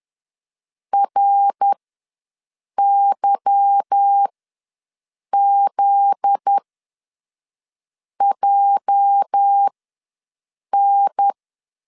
Morse code, heard as a keyed tone, about 10 wpm, spaced as for 5 wpm, 800 Hz. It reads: RYZJN
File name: LEZIONI / 8 YJ45 / 8 YJ4 ws 10wpm 4020